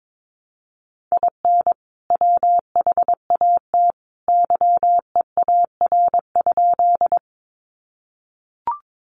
Morse code, heard as a keyed tone, about 22 wpm, 700 Hz. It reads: IN WHAT YEAR?
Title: Sentences from Top 200 Words 22wpm